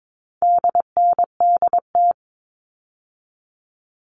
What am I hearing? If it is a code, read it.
Morse code, heard as a keyed tone, about 22 wpm, 700 Hz. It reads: DNDT
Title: All Cut Numbers - 4-Digit 22wpm